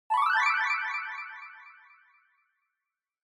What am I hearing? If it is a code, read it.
There is no code.